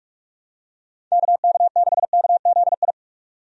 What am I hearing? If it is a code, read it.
Morse code, heard as a keyed tone, about 46 wpm, 700 Hz. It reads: KK6KZI
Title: USA Call Signs - 6 Characters - 45wpm